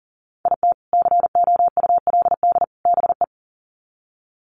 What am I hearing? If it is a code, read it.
Morse code, heard as a keyed tone, about 40 wpm, 700 Hz. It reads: IT COULD BE
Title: Sets of 3 Words 40wpm